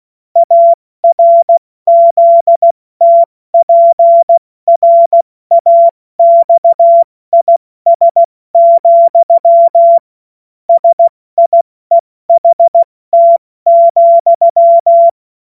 Morse code, heard as a keyed tone, about 16 wpm, 680 Hz. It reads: ARZTPRAXIS, SIEHT,
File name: Morsekurs / Witze / Witz 12 März 2024 70 F80